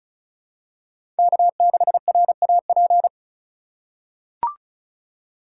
Morse code, heard as a keyed tone, about 36 wpm, 700 Hz. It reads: K6RAP
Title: USA Call Signs - 5 Characters - 4x Character Spacing - 35wpm